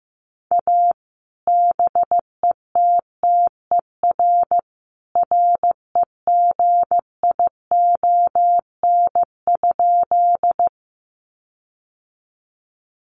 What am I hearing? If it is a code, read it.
Morse code, heard as a keyed tone, about 15 wpm, 700 Hz. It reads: A BETTER REGION?
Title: Sentences from Top 1500 Words 15wpm